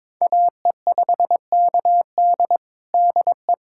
Morse code, heard as a keyed tone, about 22 wpm, 700 Hz. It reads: AE5KD DE
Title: USA Callsign DE Callsign - 5 Characters - 22wpm